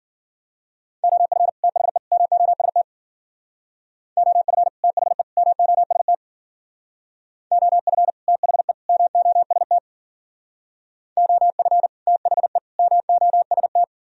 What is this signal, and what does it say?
Morse code, heard as a keyed tone, about 53 wpm, 700 Hz. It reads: OF THE MOST OF THE MOST OF THE MOST OF THE MOST